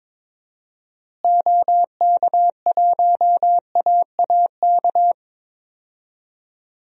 Morse code, heard as a keyed tone, about 22 wpm, 700 Hz. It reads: OK1AAK
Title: DXCC-100 Call Signs - 6 Characters - Repeated 2x - 22wpm